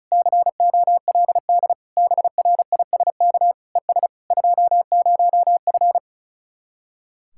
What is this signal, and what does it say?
Morse code, heard as a keyed tone, about 44 wpm, 700 Hz. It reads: COLD BRISK ES 20F
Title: Speed-Racing Common QSO Phrases - 35wpm